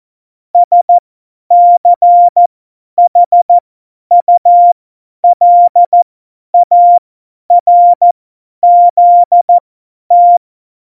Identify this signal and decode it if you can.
Morse code, heard as a keyed tone, about 14 wpm, spaced as for 10 wpm, 700 Hz. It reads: SCHULARZT
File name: Morsekurs / Witze / Witz 05. Dezember 2017, 50 F70